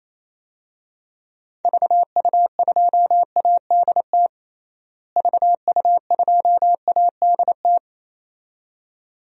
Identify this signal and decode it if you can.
Morse code, heard as a keyed tone, about 28 wpm, 700 Hz. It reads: VU2ADT VU2ADT
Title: DXCC-100 Call Signs - 6 Characters - Repeated 2x - 28wpm